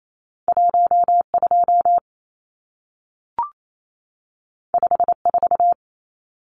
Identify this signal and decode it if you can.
Morse code, heard as a keyed tone, about 28 wpm, 700 Hz. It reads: 12 54
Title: Numbers - 2-Digit 28wpm